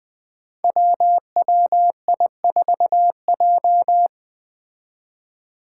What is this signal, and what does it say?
Morse code, heard as a keyed tone, about 20 wpm, 700 Hz. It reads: WWI4J
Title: Five Challenging Letter-Number Combinations 20wpm